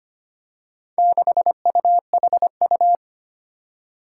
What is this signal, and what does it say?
Morse code, heard as a keyed tone, about 25 wpm, 700 Hz. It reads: 6UHU
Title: Four Challenging Letter-Number Combinations 25wpm